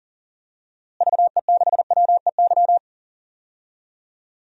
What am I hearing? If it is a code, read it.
Morse code, heard as a keyed tone, about 37 wpm, 700 Hz. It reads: VE6WEY
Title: DXCC-100 Call Signs - 6 Characters - Repeated 2x - 40wpm